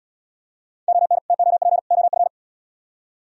Morse code, heard as a keyed tone, about 69 wpm, 700 Hz. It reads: ON TOP OF